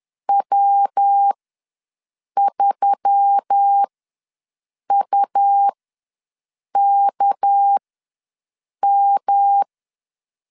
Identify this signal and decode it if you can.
Morse code, heard as a keyed tone, about 10 wpm, spaced as for 6 wpm, 800 Hz. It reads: W3UKM
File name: LEZIONI / 6 PKX0 / 6 PKX ws 10wpm 3706